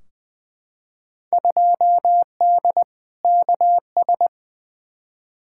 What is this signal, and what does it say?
Morse code, heard as a keyed tone, about 20 wpm, 700 Hz. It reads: 2D KS
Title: Field Day Exchanges Encore - 20wpm